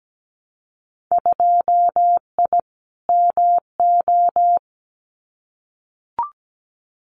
Morse code, heard as a keyed tone, about 17 wpm, 700 Hz. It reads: 2I MO
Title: Winter Field Day Exchanges 17wpm